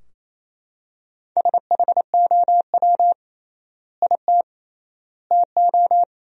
Morse code, heard as a keyed tone, about 28 wpm, 700 Hz. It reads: SHOW IT TO
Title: 3x Word-Spacing - Sentences from Top 200 Words - 28wpm